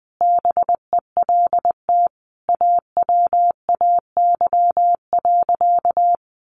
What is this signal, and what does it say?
Morse code, heard as a keyed tone, about 20 wpm, 700 Hz. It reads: BELT AWAY.
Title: Sentences from Top 1400 Words 20wpm